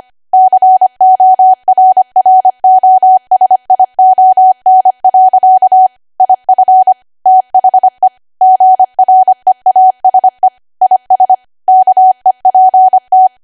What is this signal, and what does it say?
Morse code, heard as a keyed tone, about 25 wpm, 750 Hz. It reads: CORROSION. IF THE GREASE IS KEPT
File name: Archive / 25 WPM / 25WPM